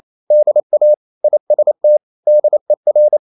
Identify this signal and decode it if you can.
Morse code, heard as a keyed tone, about 28 wpm, 600 Hz. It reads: DA IST DER